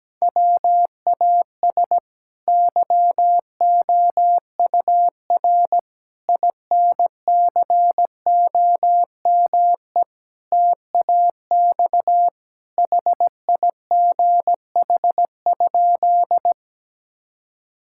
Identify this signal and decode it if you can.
Morse code, heard as a keyed tone, about 17 wpm, 700 Hz. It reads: WAS YOUR INCOME TAX HIGH?
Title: Sentences from Top 1000 Words - Review Part 5 - 17wpm